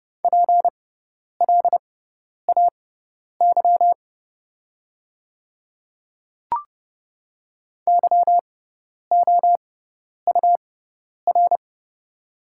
Morse code, heard as a keyed tone, about 30 wpm, spaced as for 10 wpm, 690 Hz. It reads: PLAY YOUR
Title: 6x Character Spacing - Top 200 Words - 30wpm